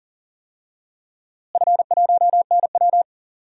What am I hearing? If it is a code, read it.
Morse code, heard as a keyed tone, about 40 wpm, 700 Hz. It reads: F1NW
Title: DXCC-100 Call Signs - 4 Characters - Repeated 2x - 40wpm